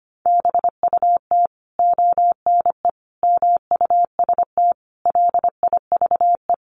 Morse code, heard as a keyed tone, about 25 wpm, 700 Hz. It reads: BUT ONE MUST LIVE
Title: Sentences from Top 1400 Words 25wpm